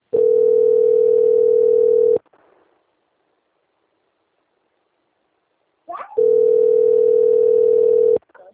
Ringback tone (North America), 440 + 480 Hz.
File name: out- - - - - .30078